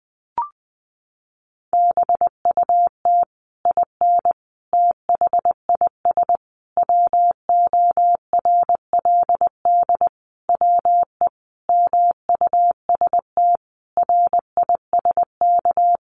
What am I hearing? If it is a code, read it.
Morse code, heard as a keyed tone, about 20 wpm, 700 Hz. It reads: BUT IN THIS WORLD WE MUST RISK
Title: Sentences from Top 600 Words 20wpm